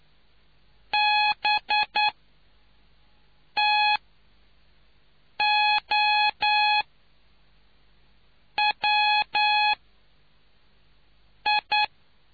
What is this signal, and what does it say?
Morse code, heard as a keyed tone, about 9 wpm, spaced as for 4 wpm, 840 Hz. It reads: BTOWI